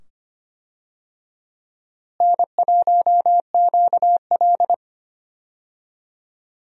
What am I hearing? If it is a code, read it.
Morse code, heard as a keyed tone, about 25 wpm, 700 Hz. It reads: N1QL
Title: USA Call Signs - 4 Characters - Repeated 2x - 25wpm